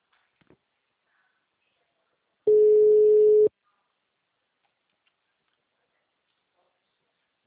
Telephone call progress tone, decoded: ringback tone

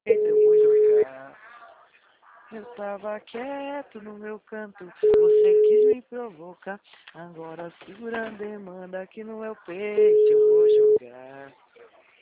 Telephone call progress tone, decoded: ringback tone

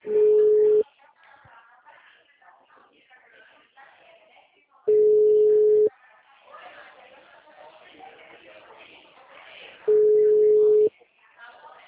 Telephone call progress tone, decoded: ringback tone

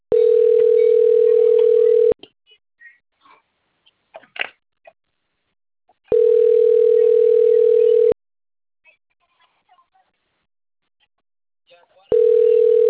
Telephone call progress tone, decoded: ringback tone

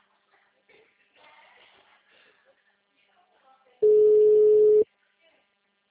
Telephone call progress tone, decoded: ringback tone

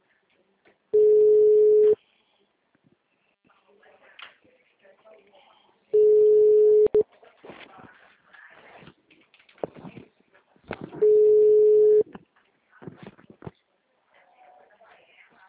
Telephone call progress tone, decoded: ringback tone

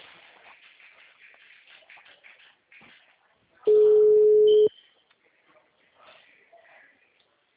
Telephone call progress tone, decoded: ringback tone

